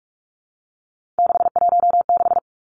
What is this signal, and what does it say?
Morse code, heard as a keyed tone, about 46 wpm, 700 Hz. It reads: 616
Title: Three Challenging Letter-Number Combinations 45wpm